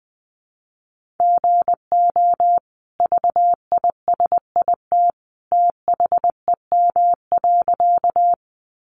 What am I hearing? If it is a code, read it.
Morse code, heard as a keyed tone, about 20 wpm, 700 Hz. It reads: GO VISIT THEM.
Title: Sentences from Top 900 Words 20wpm